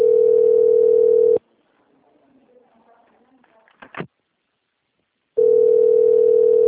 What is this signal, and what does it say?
Ringback tone (North America), 440 + 480 Hz, with speech.
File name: out- - - - - .51